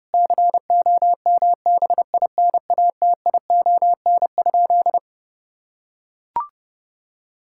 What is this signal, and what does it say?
Morse code, heard as a keyed tone, about 30 wpm, 700 Hz. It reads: COMBINATION?
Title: Sentences from Top 1000 Words - Review Part 6 - 30wpm